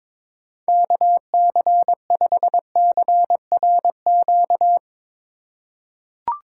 Morse code, heard as a keyed tone, about 22 wpm, 700 Hz. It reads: KC5CRQ
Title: USA Call Signs - 6 Characters - 4x Character Spacing - 22wpm